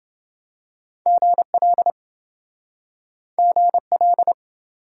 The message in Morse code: GL GL

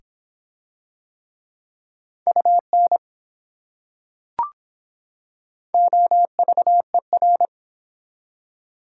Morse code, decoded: UN OVER

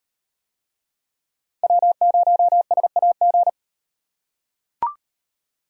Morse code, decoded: W0SAG

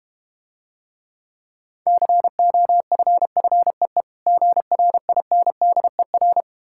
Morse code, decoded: COFFEE GRINDER

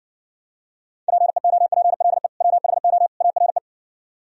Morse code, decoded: PEOPLE WHO ARE

